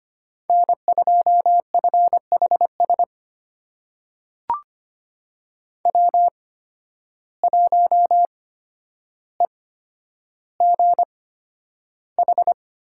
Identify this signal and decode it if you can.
Morse code, heard as a keyed tone, about 25 wpm, spaced as for 11 wpm, 700 Hz. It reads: N2FHS W1EGH
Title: USA Call Signs - 5 Characters - 8x Character Spacing - 25wpm